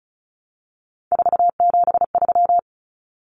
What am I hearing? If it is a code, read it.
Morse code, heard as a keyed tone, about 34 wpm, 700 Hz. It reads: 473
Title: Numbers - 3-Digit 35wpm